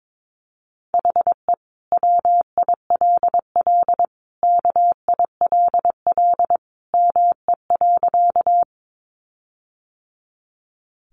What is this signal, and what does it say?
Morse code, heard as a keyed tone, about 22 wpm, 700 Hz. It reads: HE WILL KILL ME.
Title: Sentences from Top 1400 Words 22wpm